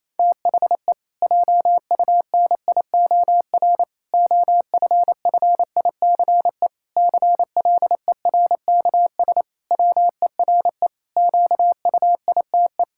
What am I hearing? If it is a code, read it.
Morse code, heard as a keyed tone, about 28 wpm, 700 Hz. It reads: THE JUNIOR OFFICE CLERKS WERE QUITE
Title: Pangram Challenge v2 - 28wpm